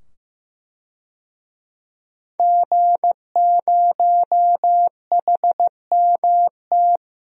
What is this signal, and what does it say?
Morse code, heard as a keyed tone, about 15 wpm, 700 Hz. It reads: G0HMT
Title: DXCC-100 Call Signs - 5 Characters - 15wpm